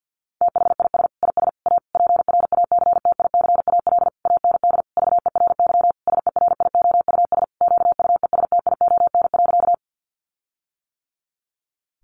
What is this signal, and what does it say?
Morse code, heard as a keyed tone, about 50 wpm, 700 Hz. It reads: THIS IS A PRACTICAL AND VERY SERIOUS QUESTION.
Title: Sentences from Top 1400 Words 50wpm